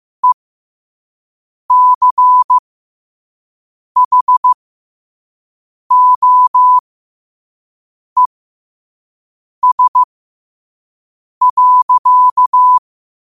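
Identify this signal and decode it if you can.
Morse code, heard as a keyed tone, about 15 wpm, spaced as for 5 wpm, 1000 Hz. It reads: ECHOES.
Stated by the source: Morse code quote of the day at 5 wpm.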